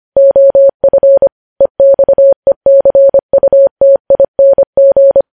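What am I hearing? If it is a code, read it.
Morse code, heard as a keyed tone, about 25 wpm, 550 Hz. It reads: OF EXECUTING
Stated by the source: Morse code practice of news headlines sent at 25 wpm.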